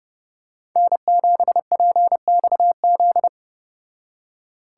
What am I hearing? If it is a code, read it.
Morse code, heard as a keyed tone, about 30 wpm, 700 Hz. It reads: N7PXZ